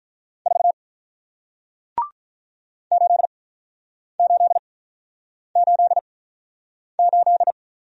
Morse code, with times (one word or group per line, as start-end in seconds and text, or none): 0.5-0.7 4
2.9-7.5 8888